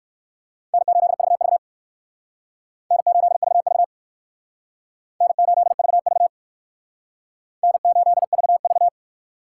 Morse code, heard as a keyed tone, about 56 wpm, 700 Hz. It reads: N8VV N8VV N8VV N8VV